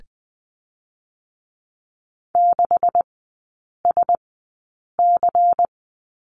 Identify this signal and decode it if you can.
Morse code, heard as a keyed tone, about 20 wpm, spaced as for 8 wpm, 700 Hz. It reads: 6SC
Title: 127 Lesson 21e - Number 6 - Three Characters 20wpm